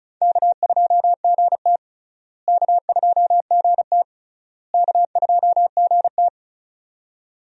Morse code, heard as a keyed tone, about 34 wpm, 700 Hz. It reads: K2GT K2GT K2GT
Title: USA Call Signs - 4 Characters - Repeated 3x - 35wpm